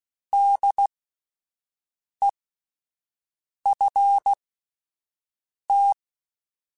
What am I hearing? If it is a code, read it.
Morse code, heard as a keyed tone, about 16 wpm, spaced as for 5 wpm, 800 Hz. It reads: DEFT